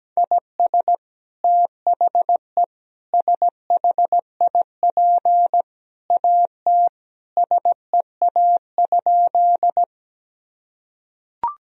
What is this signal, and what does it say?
Morse code, heard as a keyed tone, about 17 wpm, 700 Hz. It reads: IS THE SHIP AT SEA?
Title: Sentences from Top 1000 Words - Review Part 2 - 17wpm